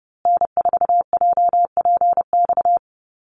Morse code, heard as a keyed tone, about 30 wpm, 700 Hz. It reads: N4JPX
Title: USA Call Signs Repeated 3x 30wpm